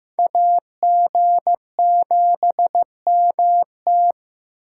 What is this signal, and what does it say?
Morse code, heard as a keyed tone, about 18 wpm, 700 Hz. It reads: AG7MT
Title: Speed Racing - USA 2x2 Call Signs 15wpm